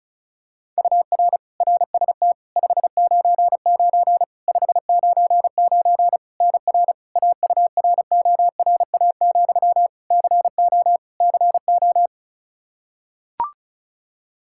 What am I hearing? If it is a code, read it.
Morse code, heard as a keyed tone, about 36 wpm, 700 Hz. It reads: UR RST 599 599 NR AURORA, CO CO
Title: K1USN SST QSO Practice 35wpm